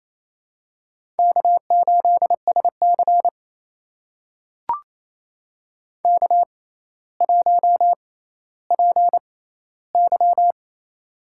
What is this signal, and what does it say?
Morse code, heard as a keyed tone, about 28 wpm, spaced as for 14 wpm, 700 Hz. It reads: K8SC K1PY
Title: USA Call Signs - 4 Characters - 6x Character Spacing - 28wpm